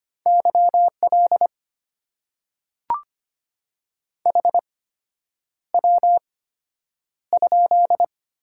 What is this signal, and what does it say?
Morse code, heard as a keyed tone, about 25 wpm, spaced as for 9 wpm, 700 Hz. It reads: YL HW?
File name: Top 90 QSO Elements - 8x Character Spacing - 25wpm